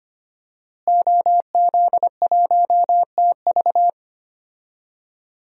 Morse code, heard as a keyed tone, about 25 wpm, 700 Hz. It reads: OZ1TV
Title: DXCC-100 Call Signs - 5 Characters - Repeated 3x - 25wpm